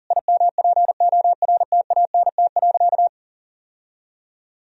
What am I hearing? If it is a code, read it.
Morse code, heard as a keyed tone, about 40 wpm, 700 Hz. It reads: IMPORTANT.
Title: Sentences from Top 1000 Words - Review Part 3 - 40wpm